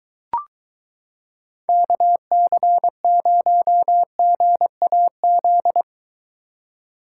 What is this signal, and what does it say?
Morse code, heard as a keyed tone, about 23 wpm, 700 Hz. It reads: KC0GAZ